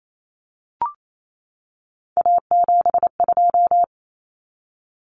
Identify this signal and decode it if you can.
Morse code, heard as a keyed tone, about 28 wpm, 700 Hz. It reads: A72
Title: Common Cut Numbers - 3-Digit 28wpm